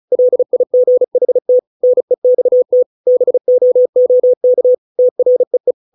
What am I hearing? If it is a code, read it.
Morse code, heard as a keyed tone, about 34 wpm, 500 Hz. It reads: LIGHT NEXT BOOK TREE